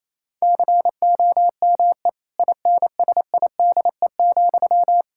Morse code, heard as a keyed tone, about 28 wpm, 700 Hz. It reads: COME INSIDE,
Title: Sentences from Top 1000 Words - Review Part 6 - 28wpm